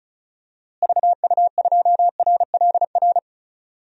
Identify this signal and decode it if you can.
Morse code, heard as a keyed tone, about 36 wpm, 700 Hz. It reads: VU2RLR
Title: DXCC-100 Call Signs - 6 Characters - Repeated 3x - 35wpm